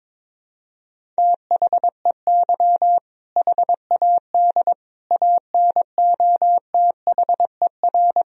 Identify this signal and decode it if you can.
Morse code, heard as a keyed tone, about 22 wpm, 700 Hz. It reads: THEY HAD ANOTHER
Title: Sentences from Top 1000 Words - Review Part 1 - 22wpm